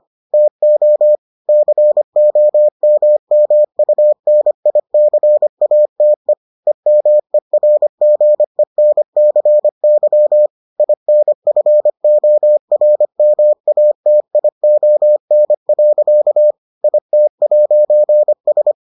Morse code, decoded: TO COMMUNICATE EMERGENCY INFORMATION. IT'S

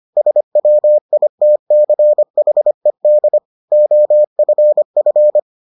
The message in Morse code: SWITCHED OFF

